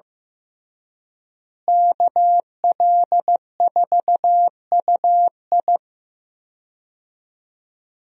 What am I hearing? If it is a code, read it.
Morse code, heard as a keyed tone, about 15 wpm, 700 Hz. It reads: KL4UI
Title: USA Call Signs - 5 Characters - Repeated 3x - 15wpm